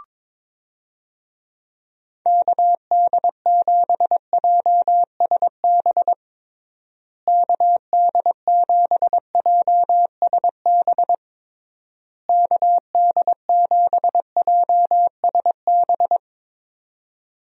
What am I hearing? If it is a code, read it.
Morse code, heard as a keyed tone, about 22 wpm, 700 Hz. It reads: KD7JSB KD7JSB KD7JSB
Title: USA Call Signs - 6 Characters - Repeated 3x - 22wpm